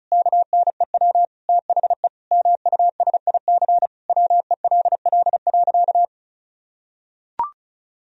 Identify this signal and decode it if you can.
Morse code, heard as a keyed tone, about 36 wpm, 700 Hz. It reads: KNEW THE MUSIC WELL.
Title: Sentences from Top 1000 Words - Review Part 1 - 35wpm